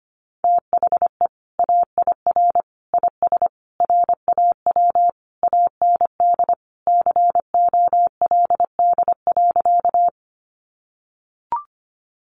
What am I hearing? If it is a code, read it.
Morse code, heard as a keyed tone, about 25 wpm, 700 Hz. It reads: THE AIR IS RAW AND COLD.
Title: Sentences from Top 1500 Words 25wpm